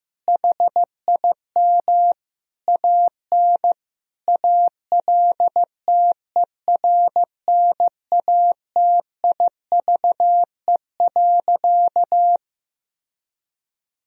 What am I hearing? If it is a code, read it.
Morse code, heard as a keyed tone, about 15 wpm, 700 Hz. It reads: HIM AN ALTERNATIVE.
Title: Sentences from Top 1000 Words - Review Part 4 - 15wpm